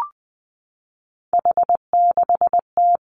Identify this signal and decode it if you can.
Morse code, heard as a keyed tone, about 19 wpm, 700 Hz. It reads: H6T
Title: Three Challenging Letter-Number Combinations 20wpm